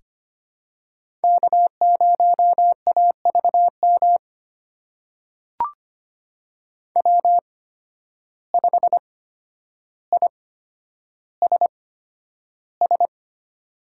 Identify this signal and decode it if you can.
Morse code, heard as a keyed tone, about 25 wpm, spaced as for 11 wpm, 700 Hz. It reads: K0AVM W5ISS
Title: USA Call Signs - 5 Characters - 8x Character Spacing - 25wpm